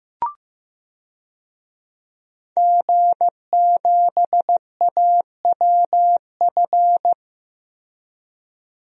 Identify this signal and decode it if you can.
Morse code, heard as a keyed tone, about 15 wpm, 700 Hz. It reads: G7AWF
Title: DXCC-100 Call Signs - 5 Characters - Repeated 2x - 15wpm